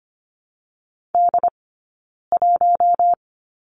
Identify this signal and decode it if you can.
Morse code, heard as a keyed tone, about 25 wpm, 700 Hz. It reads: D 1
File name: course / 25:10wpm / 092 Lesson 16d - Letter c - Two Characters 25wpm